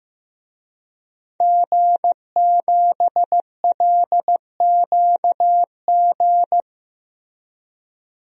Morse code, heard as a keyed tone, about 16 wpm, 700 Hz. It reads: G7LQG
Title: DXCC-100 Call Signs - 5 Characters - Repeated 2x - 15wpm